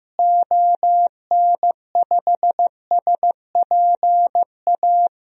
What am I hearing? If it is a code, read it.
Morse code, heard as a keyed tone, about 15 wpm, 700 Hz. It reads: ON5SPA
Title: DXCC-100 Call Signs - 6 Characters - Repeated 2x - 15wpm